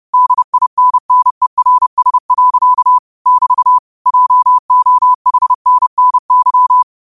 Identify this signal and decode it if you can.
Morse code, heard as a keyed tone, about 30 wpm, 1000 Hz. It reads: DINNERS. <BT> JOHNNY